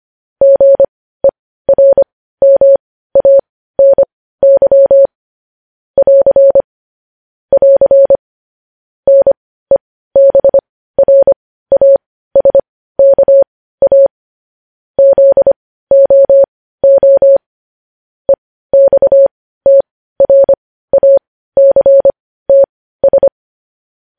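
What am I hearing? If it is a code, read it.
Morse code, heard as a keyed tone, about 25 wpm, spaced as for 15 wpm, 550 Hz. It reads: GERMANY <AR> <AR> NEBRASKA ZOO EXTRACTS
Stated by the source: Morse news headlines at 15 wpm.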